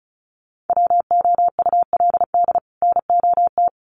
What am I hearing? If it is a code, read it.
Morse code, heard as a keyed tone, about 34 wpm, 700 Hz. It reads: WOULD NOT
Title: Sets of 2 Words 35wpm